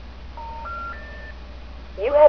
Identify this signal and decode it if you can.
Special information tone, with speech.